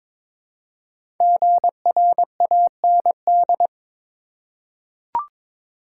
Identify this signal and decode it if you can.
Morse code, heard as a keyed tone, about 23 wpm, 700 Hz. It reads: GRAND
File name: 5 Letter Words - First 3 Letters Given - 22wpm